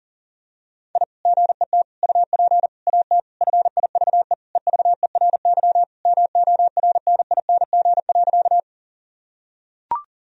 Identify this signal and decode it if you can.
Morse code, heard as a keyed tone, about 40 wpm, 700 Hz. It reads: I GET UP AT FIVE EVERY MORNING.
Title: Sentences from Top 1000 Words - Review Part 2 - 40wpm